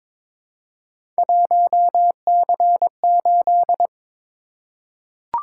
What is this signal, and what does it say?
Morse code, heard as a keyed tone, about 22 wpm, 700 Hz. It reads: 1C8
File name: practice / 22wpm / Three Letter-Number Combinations - 22wpm